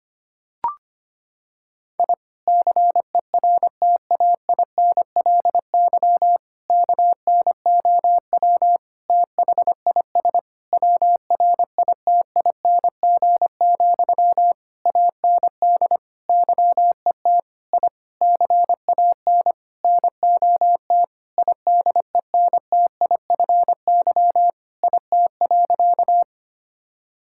Morse code, heard as a keyed tone, about 25 wpm, 700 Hz. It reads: I CERTAINLY KNOW THIS WRITING, AND YET I CAN NOT IDENTIFY IT.
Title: Sentences from Top 1000 Words - Review Part 6 - 25wpm